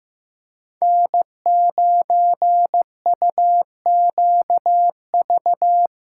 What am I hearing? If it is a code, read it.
Morse code, heard as a keyed tone, about 17 wpm, 700 Hz. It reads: N9UQV
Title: Speed Racing - USA 1x3 Call Signs 15wpm